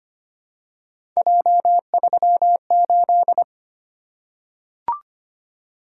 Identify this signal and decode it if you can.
Morse code, heard as a keyed tone, about 25 wpm, 700 Hz. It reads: J38
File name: Three Letter-Number Combinations - 25wpm